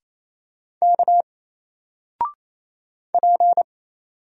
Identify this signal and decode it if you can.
Morse code, heard as a keyed tone, about 28 wpm, 700 Hz. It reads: K P
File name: Single-Letter-Number-28wpm